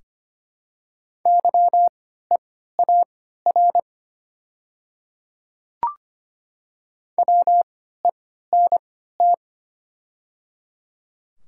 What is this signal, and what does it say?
Morse code, heard as a keyed tone, about 25 wpm, spaced as for 14 wpm, 700 Hz. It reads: YEAR WENT
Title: 3x Character Spacing - Top 300 Words - 25wpm